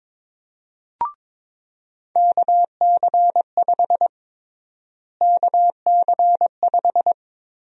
Morse code, heard as a keyed tone, about 22 wpm, 700 Hz. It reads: KC5 KC5
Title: USA Call Sign Prefixes - Repeated 2x - 22wpm